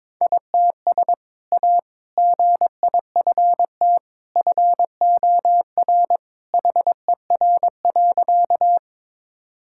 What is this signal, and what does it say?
Morse code, heard as a keyed tone, about 21 wpm, 700 Hz. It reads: ITS A GIFT FOR HER.
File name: practice / 22wpm / Sentences with 2-Word Phrases - 22wpm